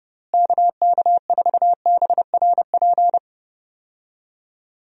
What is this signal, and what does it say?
Morse code, heard as a keyed tone, about 30 wpm, 700 Hz. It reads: KK4BRP
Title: USA Call Signs - 6 Characters - Repeated 2x - 30wpm